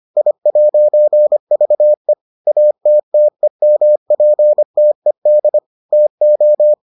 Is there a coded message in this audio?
Yes: Morse code from 0.2 to 6.7 s, heard as a keyed tone, about 25 wpm, 600 Hz. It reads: I'VE ATTEMPTED TO